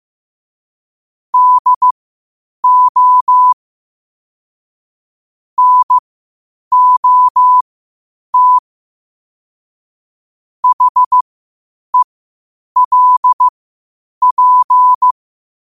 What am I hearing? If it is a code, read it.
Morse code, heard as a keyed tone, about 15 wpm, spaced as for 8 wpm, 1000 Hz. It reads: DO NOT HELP
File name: qotdfiles / 8wpm